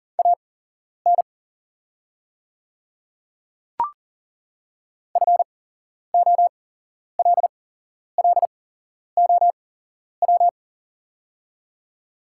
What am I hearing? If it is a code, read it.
Morse code, heard as a keyed tone, about 40 wpm, spaced as for 11 wpm, 700 Hz. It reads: AN FOLLOW
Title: 8x Character Spacing - Top 200 Words - 40wpm